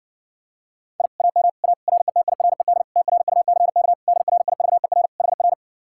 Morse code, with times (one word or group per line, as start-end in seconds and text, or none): 1.0-1.1 I
1.2-1.5 AM
1.6-1.7 A
1.9-2.8 RETIRED
3.0-3.9 TRUCK
4.1-5.1 DRIVER
5.2-5.5 HR